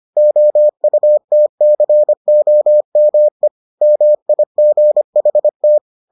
Morse code, heard as a keyed tone, about 25 wpm, 600 Hz. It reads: OUTCOME MIGHT